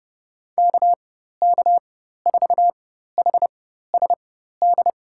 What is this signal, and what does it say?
Morse code, heard as a keyed tone, about 30 wpm, spaced as for 14 wpm, 700 Hz. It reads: KK4HSD
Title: USA Call Signs - 6 Characters - 4x Character Spacing - 30wpm